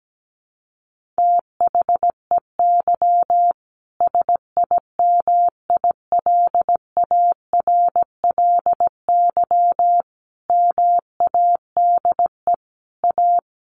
Morse code, heard as a keyed tone, about 17 wpm, 700 Hz. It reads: THEY SIMILARLY MADE A
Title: Sentences from Top 1400 Words 17wpm